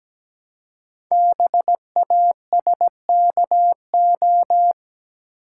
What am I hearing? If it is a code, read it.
Morse code, heard as a keyed tone, about 17 wpm, 700 Hz. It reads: BASKO
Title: Code Groups - 5 Letters - 17wpm